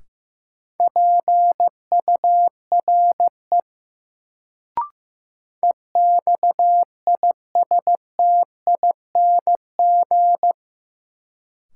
Morse code, heard as a keyed tone, about 15 wpm, 700 Hz. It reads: PURE EXISTING
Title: Sentences from Top 1500 Words - New Word Drill - 15wpm